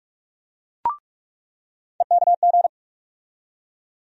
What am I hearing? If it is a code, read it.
Morse code, heard as a keyed tone, about 44 wpm, 700 Hz. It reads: EKG
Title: USA Call Sign Postfixes - 3 Letters 45wpm